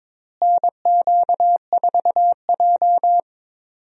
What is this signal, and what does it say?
Morse code, heard as a keyed tone, about 21 wpm, 700 Hz. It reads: NQ4J
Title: USA Call Signs - 4 Characters - Repeated 3x - 22wpm